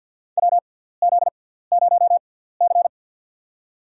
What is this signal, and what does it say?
Morse code, heard as a keyed tone, about 47 wpm, spaced as for 17 wpm, 700 Hz. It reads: WZ0C
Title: USA Call Signs - 4 Characters - 6x Character Spacing - 50wpm